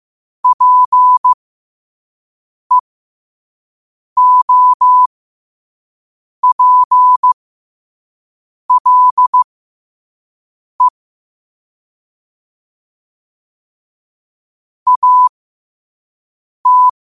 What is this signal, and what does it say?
Morse code, heard as a keyed tone, about 15 wpm, spaced as for 5 wpm, 1000 Hz. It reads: PEOPLE AT
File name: qotdfiles / 5wpm